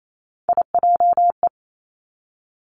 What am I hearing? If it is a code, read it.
Morse code, heard as a keyed tone, about 28 wpm, 700 Hz. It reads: IJE
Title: Three Challenging Letter-Number Combinations 28wpm